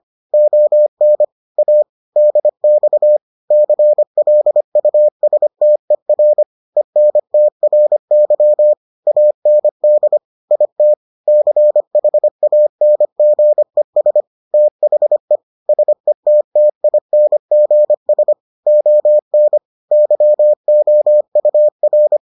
Morse code, heard as a keyed tone, about 25 wpm, 600 Hz. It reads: ON A DX CLUSTER ENTRY AND IT CHANGES THE SETTINGS ON YOUR